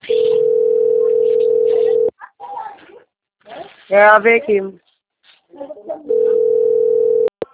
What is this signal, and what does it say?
Ringback tone (North America), 440 + 480 Hz, with speech.
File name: out- - - - - .12221